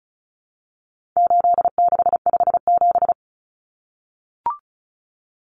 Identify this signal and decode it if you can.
Morse code, heard as a keyed tone, about 34 wpm, 700 Hz. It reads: 8657
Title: Common Cut Numbers - 4-Digit 35wpm